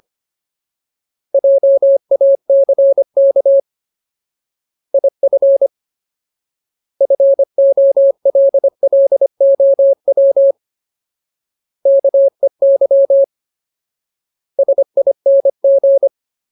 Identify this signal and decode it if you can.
Morse code, heard as a keyed tone, about 25 wpm, 550 Hz. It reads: JACK IF FOLLOW KEY SING
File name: Ultimate Word List / 25WPM Ultimate